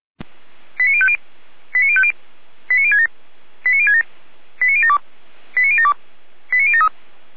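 Five-tone selective call (ZVEI), heard as quick runs of five tones, tones of about 70 ms. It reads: 89050 89050 89067 89067 89072 89072 89073
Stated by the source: ZVEI five-tone selective call received on SDR.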